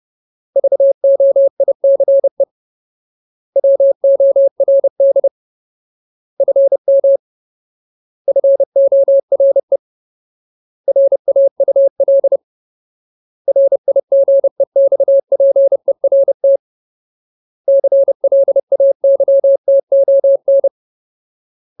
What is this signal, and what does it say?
Morse code, heard as a keyed tone, about 30 wpm, 550 Hz. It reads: VOICE WORD FM FORE RAUL RIGEXPERT CLAYTON
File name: Ultimate Word List / 30WPM Ultimate